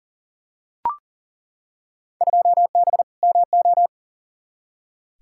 Morse code, heard as a keyed tone, about 40 wpm, 700 Hz. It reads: 2B MO
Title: Field Day Exchanges Encore - 40wpm